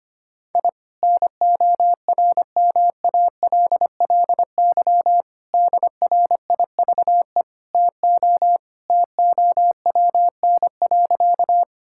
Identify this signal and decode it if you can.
Morse code, heard as a keyed tone, about 25 wpm, 700 Hz. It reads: I NORMALLY DRIVE TO TOWN.
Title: Sentences from Top 1000 Words - Review Part 6 - 25wpm